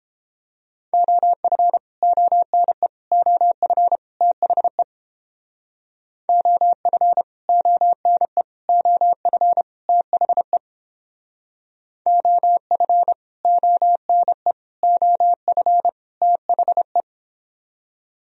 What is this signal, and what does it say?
Morse code, heard as a keyed tone, about 30 wpm, 700 Hz. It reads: OF ONE OF THE OF ONE OF THE OF ONE OF THE